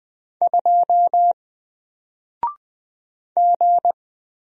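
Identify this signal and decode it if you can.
Morse code, heard as a keyed tone, about 20 wpm, 700 Hz. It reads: 2 G
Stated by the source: Single letter and number Morse drill at 20 wpm, from N4LCD.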